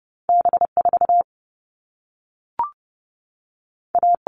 Morse code, heard as a keyed tone, about 30 wpm, 700 Hz. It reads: B 4 A
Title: All Cut Numbers - 2-Digit 30wpm